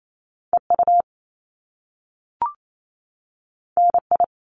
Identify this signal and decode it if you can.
Morse code, heard as a keyed tone, about 28 wpm, 700 Hz. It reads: EU NI
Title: Top Two Letter Combinations in Words 28wpm